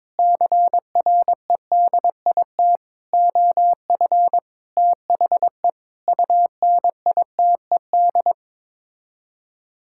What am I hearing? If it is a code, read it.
Morse code, heard as a keyed tone, about 21 wpm, 700 Hz. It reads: CREDIT OF THE UNITED
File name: practice / 22wpm / Sets of 8 Words - 22wpm